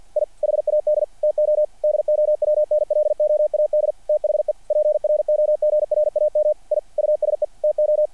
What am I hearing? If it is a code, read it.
Morse code, heard as a keyed tone, about 48 wpm, 600 Hz. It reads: I HAD TO DOWNLOAD THE PROGRAM I USE TO
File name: archived / CRSnet